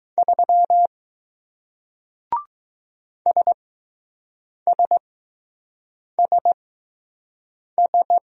Morse code, heard as a keyed tone, about 20 wpm, spaced as for 6 wpm, 700 Hz. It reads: 3 SSSS